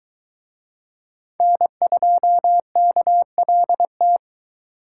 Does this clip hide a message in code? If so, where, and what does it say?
Yes: Morse code from 1.4 to 4.2 s, heard as a keyed tone, about 22 wpm, 700 Hz. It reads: N2KLT